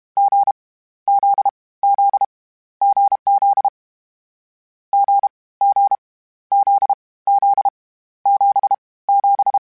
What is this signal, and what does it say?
Morse code, heard as a keyed tone, about 32 wpm, 800 Hz. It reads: G Z Z GZ G G Z Z 7 7
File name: Difficult Combos / Difficult Combos-mp3s / awj1 ndb6 gz7 @30